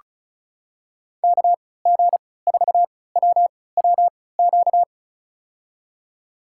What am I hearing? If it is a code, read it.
Morse code, heard as a keyed tone, about 36 wpm, spaced as for 20 wpm, 700 Hz. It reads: KG4WWQ
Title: USA Call Signs - 6 Characters - 3x Character Spacing - 35wpm